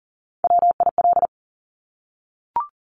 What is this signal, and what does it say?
Morse code, heard as a keyed tone, about 41 wpm, 690 Hz. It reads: WIL
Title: Top Three Letter Combinations in Words 40wpm